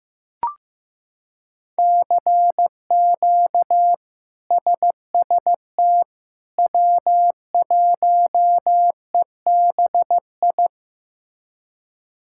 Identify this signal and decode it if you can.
Morse code, heard as a keyed tone, about 15 wpm, 700 Hz. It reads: CQ SST W1EBI
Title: K1USN SST QSO Practice - 15wpm